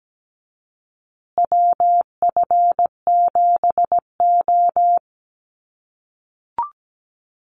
Morse code, heard as a keyed tone, about 17 wpm, 700 Hz. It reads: WF7O
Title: USA Call Signs Repeated 2x 17wpm